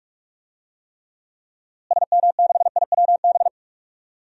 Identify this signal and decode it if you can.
Morse code, heard as a keyed tone, about 43 wpm, 700 Hz. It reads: SM6IWB
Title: DXCC-100 Call Signs - 6 Characters - Repeated 2x - 45wpm